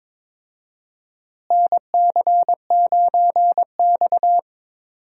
Morse code, heard as a keyed tone, about 22 wpm, 700 Hz. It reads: NC9X
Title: USA Call Signs - 4 Characters - Repeated 3x - 22wpm